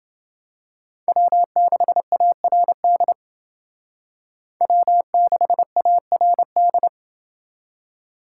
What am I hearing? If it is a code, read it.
Morse code, heard as a keyed tone, about 29 wpm, 700 Hz. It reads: W6ARD W6ARD